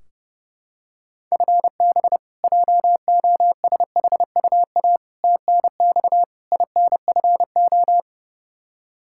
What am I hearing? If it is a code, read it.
Morse code, heard as a keyed tone, about 32 wpm, 700 Hz. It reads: FB JOSHUA TNX INFO
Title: K1USN SST QSO Practice 30wpm